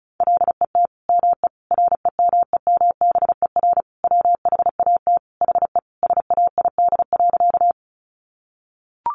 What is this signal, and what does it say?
Morse code, heard as a keyed tone, about 34 wpm, 700 Hz. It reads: LET ME REMEMBER WHAT HE SAID.
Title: Sentences from Top 400 Words 35wpm